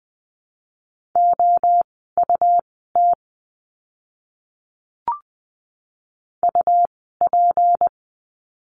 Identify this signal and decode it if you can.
Morse code, heard as a keyed tone, about 20 wpm, spaced as for 15 wpm, 700 Hz. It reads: OUT UP
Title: 160 Lesson 26f - Letter g - Words 20wpm